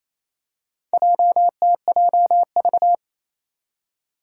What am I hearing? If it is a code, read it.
Morse code, heard as a keyed tone, about 28 wpm, 700 Hz. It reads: JTJV
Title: Four Challenging Letter-Number Combinations 28wpm